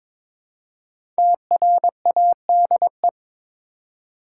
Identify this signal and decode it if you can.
Morse code, heard as a keyed tone, about 22 wpm, 700 Hz. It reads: TRADE
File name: 5 Letter Words - First 3 Letters Given - 22wpm